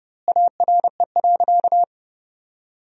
Morse code, heard as a keyed tone, about 32 wpm, 700 Hz. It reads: ARE.
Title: 2x Word-Spacing - Sentences from Top 500 Words - 30wpm